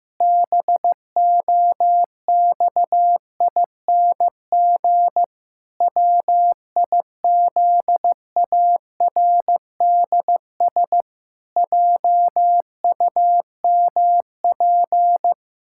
Morse code, heard as a keyed tone, about 15 wpm, 700 Hz. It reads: BOXING WIZARDS JUMP
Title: Pangram Challenge v2 - 15wpm